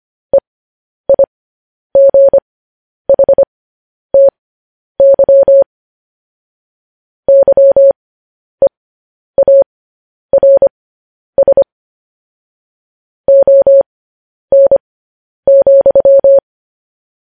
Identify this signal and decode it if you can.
Morse code, heard as a keyed tone, about 25 wpm, spaced as for 10 wpm, 550 Hz. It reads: EIGHTY YEARS ON,
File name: podcast / News headlines-1x-25x10